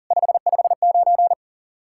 Morse code, heard as a keyed tone, about 47 wpm, 700 Hz. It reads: 559
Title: Speed-Racing Common QSO Phrases - 40wpm